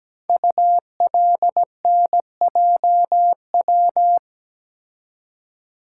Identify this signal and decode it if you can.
Morse code, heard as a keyed tone, about 17 wpm, 700 Hz. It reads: ULNJW
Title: Code Groups - 5 Letters - 17wpm